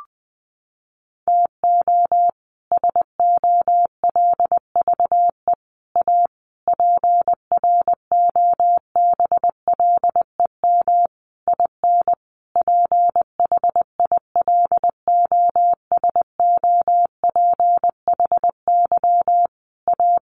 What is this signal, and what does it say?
Morse code, heard as a keyed tone, about 19 wpm, 700 Hz. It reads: TO SOLVE A PROBLEM IN PHILOSOPHY A
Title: Sentences from Top 1400 Words 20wpm